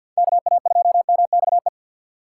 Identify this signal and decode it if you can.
Morse code, heard as a keyed tone, about 48 wpm, 690 Hz. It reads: KA2MXE
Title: USA Call Signs - 6x Character Spacing - 50wpm